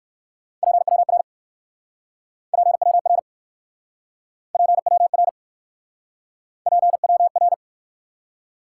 Morse code, heard as a keyed tone, about 58 wpm, 700 Hz. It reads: PWR PWR PWR PWR